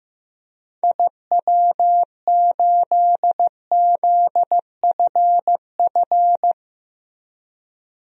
Morse code, heard as a keyed tone, about 16 wpm, 700 Hz. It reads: IW8ZFF
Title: DXCC-100 Call Signs - 6 Characters - 15wpm